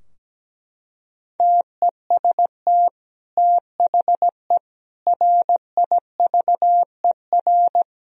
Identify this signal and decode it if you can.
Morse code, heard as a keyed tone, about 17 wpm, 700 Hz. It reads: TEST THE RIVER
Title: Sentences from Top 1000 Words - Review Part 4 - 17wpm